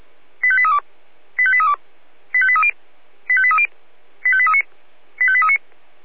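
Five-tone selective call (ZVEI), heard as quick runs of five tones, tones of about 70 ms. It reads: 85832 85832 85830 85830 85839 85839
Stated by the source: ZVEI five-tone selective call received on SDR.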